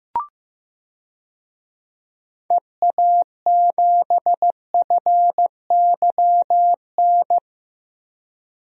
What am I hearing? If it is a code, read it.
Morse code, heard as a keyed tone, about 16 wpm, 700 Hz. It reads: EA7FYN
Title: DXCC-100 Call Signs - 6 Characters - Repeated 2x - 15wpm